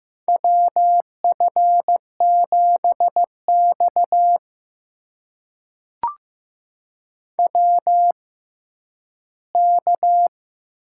Morse code, heard as a keyed tone, about 15 wpm, spaced as for 10 wpm, 700 Hz. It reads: WF7X WK
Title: USA Call Signs - 4 Characters - 6x Character Spacing - 15wpm